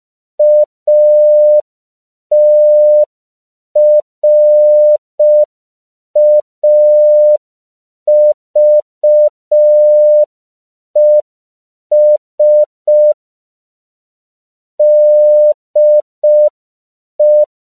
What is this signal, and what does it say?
Morse code, heard as a keyed tone, about 5 wpm, 600 Hz. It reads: ATRAVES DE